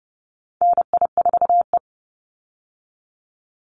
Morse code, heard as a keyed tone, about 30 wpm, 700 Hz. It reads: NI4E
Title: Call Signs - 30wpm